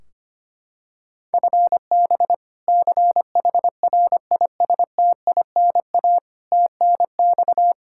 Morse code, heard as a keyed tone, about 25 wpm, 700 Hz. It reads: FB CHRISTINA TNX
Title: Practice QSO - Exchange 3 - More Info - 25wpm